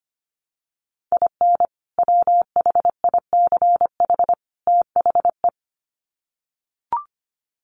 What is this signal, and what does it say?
Morse code, heard as a keyed tone, about 25 wpm, 700 Hz. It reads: IN WHICH THE
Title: Sets of 3 Words 25wpm